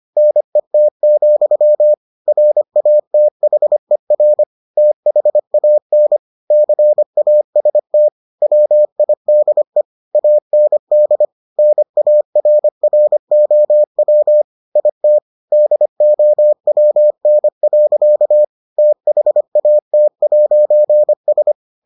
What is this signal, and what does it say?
Morse code, heard as a keyed tone, about 25 wpm, 600 Hz. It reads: NET, RATHER THAN CAST WIDE AND NARROW IT DOWN. THAT'S